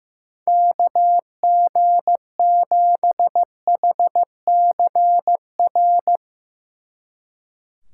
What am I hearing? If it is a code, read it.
Morse code, heard as a keyed tone, about 16 wpm, 700 Hz. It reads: KG7HCR
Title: Speed Racing - USA 2x3 Call Signs 15wpm